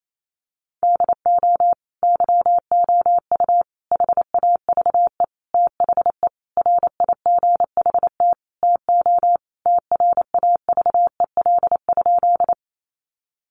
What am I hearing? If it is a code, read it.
Morse code, heard as a keyed tone, about 28 wpm, 700 Hz. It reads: DO YOU HAVE THE RIGHT TO TRAVEL?
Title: Sentences from Top 1400 Words 28wpm